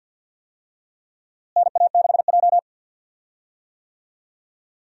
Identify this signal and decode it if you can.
Morse code, heard as a keyed tone, about 50 wpm, 700 Hz. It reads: NA6J
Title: USA Call Signs - 4 Characters - Repeated 3x - 50wpm